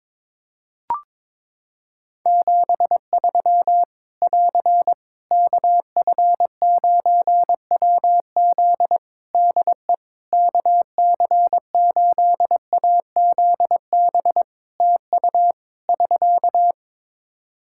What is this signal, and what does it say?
Morse code, heard as a keyed tone, about 22 wpm, 700 Hz. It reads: 73 <AR> KF9WZ DE KC8AZB TU <SK>
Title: Practice QSO - Exchange 4 - Ending - 22wpm